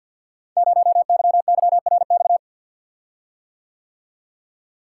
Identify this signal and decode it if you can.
Morse code, heard as a keyed tone, about 53 wpm, 700 Hz. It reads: 0YYRX
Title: Code Groups - 5 Letter:Numbers - 50wpm